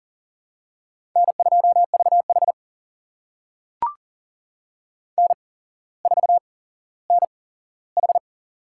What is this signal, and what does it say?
Morse code, heard as a keyed tone, about 40 wpm, spaced as for 17 wpm, 700 Hz. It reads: N2VH N4NH
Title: USA Call Signs - 4 Characters - 8x Character Spacing - 40wpm